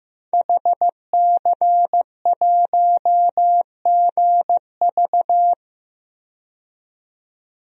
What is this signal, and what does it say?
Morse code, heard as a keyed tone, about 15 wpm, 700 Hz. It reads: HC1GV
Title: DXCC-100 Call Signs - 5 Characters - 15wpm